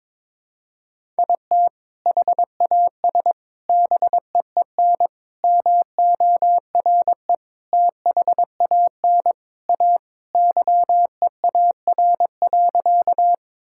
Morse code, heard as a keyed tone, about 22 wpm, 700 Hz. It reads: IT HAS BEEN MORE THAN A YEAR.
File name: practice / 22wpm / Sentences with 4-Word Phrases - 22wpm